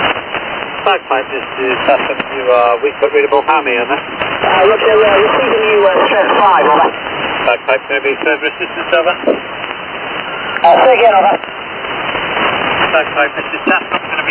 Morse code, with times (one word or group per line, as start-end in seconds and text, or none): none